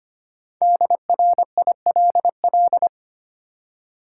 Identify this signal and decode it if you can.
Morse code, heard as a keyed tone, about 25 wpm, 690 Hz. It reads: DRILL
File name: practice / 25wpm / Easy Words with Hints - 25wpm